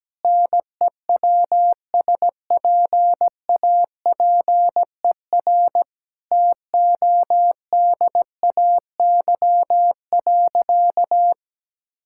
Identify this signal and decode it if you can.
Morse code, heard as a keyed tone, about 17 wpm, 700 Hz. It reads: NEWSPAPER TODAY.
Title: Sentences from Top 1000 Words - Review Part 6 - 17wpm